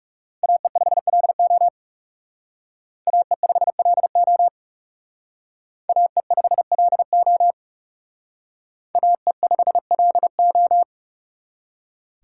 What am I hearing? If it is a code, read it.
Morse code, heard as a keyed tone, about 36 wpm, 700 Hz. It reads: AE5LO AE5LO AE5LO AE5LO
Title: Speed Racing - USA 2x2 Call Signs 30wpm